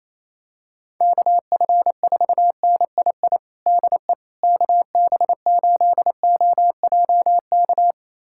Morse code, heard as a keyed tone, about 28 wpm, 700 Hz. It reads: KF4NII DE KB8OJK
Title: USA Callsign DE Callsign - 6 Characters - 28wpm